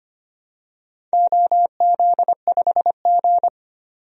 Morse code, heard as a keyed tone, about 25 wpm, 700 Hz. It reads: OZ5G
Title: DXCC-100 Call Signs - 4 Characters - 25wpm